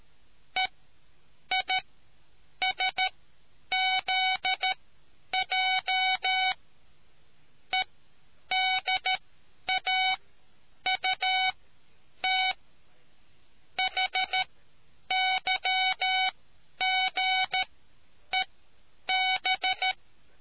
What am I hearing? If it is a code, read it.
Morse code, heard as a keyed tone, about 13 wpm, spaced as for 8 wpm, 2250 Hz. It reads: EISZJ EDAUT HYGEB